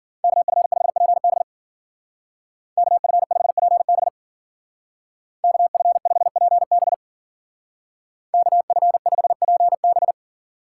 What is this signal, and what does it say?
Morse code, heard as a keyed tone, about 49 wpm, 700 Hz. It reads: KF5PB KF5PB KF5PB KF5PB
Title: Speed Racing - USA 2x2 Call Signs 40wpm